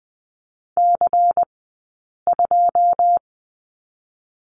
Morse code, heard as a keyed tone, about 20 wpm, spaced as for 8 wpm, 700 Hz. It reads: C2
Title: 092 Lesson 16d - Letter c - Two Characters 20wpm